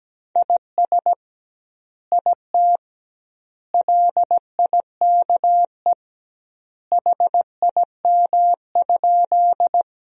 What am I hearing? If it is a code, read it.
Morse code, heard as a keyed tone, about 17 wpm, 700 Hz. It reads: IS IT LIKE HIM?
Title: 2x Word-Spacing - Sentences from Top 100 Words - 17wpm